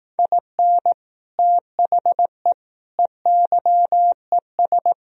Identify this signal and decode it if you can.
Morse code, heard as a keyed tone, about 18 wpm, 700 Hz. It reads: IN THE EYES
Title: Speed-Racing - Sets of 5 Words 15wpm